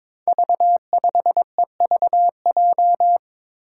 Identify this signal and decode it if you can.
Morse code, heard as a keyed tone, about 22 wpm, 700 Hz. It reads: V5EVJ
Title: Five Challenging Letter-Number Combinations 22wpm